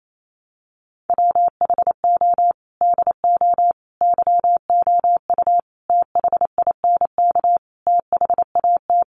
Morse code, heard as a keyed tone, about 28 wpm, 700 Hz. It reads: WHO DO YOU THINK THAT
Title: Sentences from Top 1100 Words 28wpm